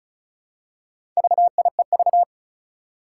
Morse code, heard as a keyed tone, about 36 wpm, 700 Hz. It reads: VIEV
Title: Four Challenging Letter-Number Combinations 35wpm